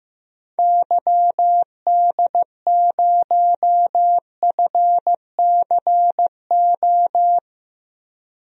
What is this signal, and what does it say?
Morse code, heard as a keyed tone, about 15 wpm, 700 Hz. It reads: YD0FCO
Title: DXCC-100 Call Signs - 6 Characters - Repeated 2x - 15wpm